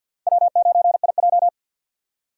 Morse code, heard as a keyed tone, about 47 wpm, 700 Hz. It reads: W9IJ
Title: USA Call Signs - 6x Character Spacing - 50wpm